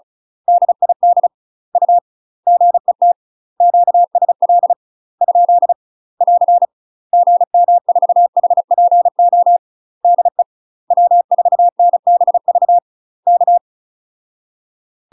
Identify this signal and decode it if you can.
Morse code, heard as a keyed tone, about 36 wpm, 700 Hz. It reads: DID U GET QSL ? <AR> GM4HPO DE W4NBV K
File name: UK MASTER QSOs TRAINING / 35WPM / UK MASTER QSOs 0007